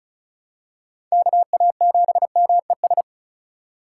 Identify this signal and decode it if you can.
Morse code, heard as a keyed tone, about 34 wpm, 700 Hz. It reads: KA7MES
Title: USA Call Signs - 6 Characters - 4x Character Spacing - 35wpm